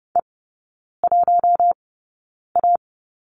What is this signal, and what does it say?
Morse code, heard as a keyed tone, about 30 wpm, spaced as for 9 wpm, 700 Hz. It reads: E1A
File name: course / 30:10wpm / 087 Lesson 15e - Number 5 - Three Characters 30wpm